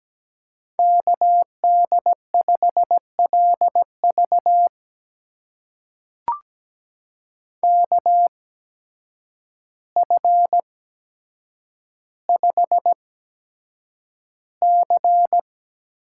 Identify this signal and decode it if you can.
Morse code, heard as a keyed tone, about 17 wpm, spaced as for 8 wpm, 700 Hz. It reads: KD5LV KF5C
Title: USA Call Signs - 5 Characters - 8x Character Spacing - 17wpm